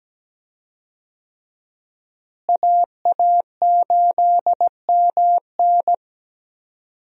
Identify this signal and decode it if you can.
Morse code, heard as a keyed tone, about 17 wpm, 700 Hz. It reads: AA8MN
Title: USA Call Signs - 5 Characters - Repeated 3x - 17wpm